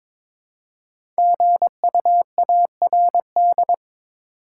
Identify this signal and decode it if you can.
Morse code, heard as a keyed tone, about 22 wpm, 700 Hz. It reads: GUARD
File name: 5 Letter Words - First 3 Letters Given - 22wpm